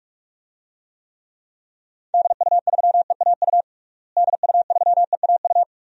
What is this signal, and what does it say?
Morse code, heard as a keyed tone, about 41 wpm, 700 Hz. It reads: DU3EAU DU3EAU
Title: DXCC-100 Call Signs - 6 Characters - Repeated 2x - 45wpm